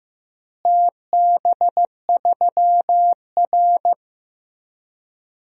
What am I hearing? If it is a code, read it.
Morse code, heard as a keyed tone, about 15 wpm, 700 Hz. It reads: TB3R
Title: DXCC-100 Call Signs - 4 Characters - Repeated 3x - 15wpm